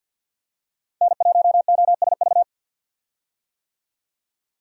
Morse code, heard as a keyed tone, about 52 wpm, 700 Hz. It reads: N1OSV
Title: USA Call Signs - 5 Characters - 50wpm